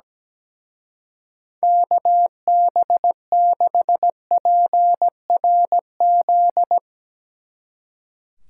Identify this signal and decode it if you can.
Morse code, heard as a keyed tone, about 17 wpm, 700 Hz. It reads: KB6PRZ
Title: Speed Racing - USA 2x3 Call Signs 17wpm